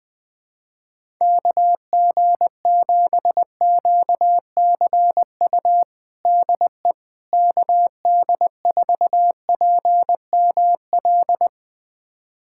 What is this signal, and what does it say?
Morse code, heard as a keyed tone, about 21 wpm, 700 Hz. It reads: KG7QCU DE KD4PML
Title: USA Callsign DE Callsign - 6 Characters - 20wpm